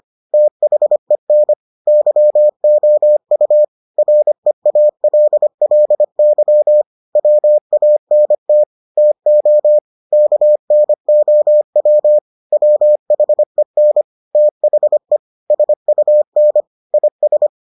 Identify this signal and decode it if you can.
Morse code, heard as a keyed tone, about 25 wpm, 600 Hz. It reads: THEN YOU REALLY WANT TO KNOW WHEN THE SUN IS